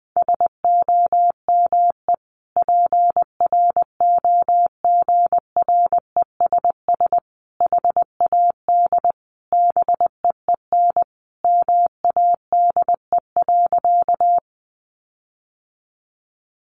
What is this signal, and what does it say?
Morse code, heard as a keyed tone, about 20 wpm, 700 Hz. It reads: SOME PROGRESS HAD BEEN MADE.
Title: Sentences from Top 1400 Words 20wpm